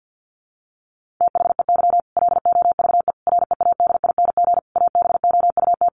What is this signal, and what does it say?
Morse code, heard as a keyed tone, about 50 wpm, 700 Hz. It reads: THEY LOVE READING ABOUT
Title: Sentences from Top 600 Words 50wpm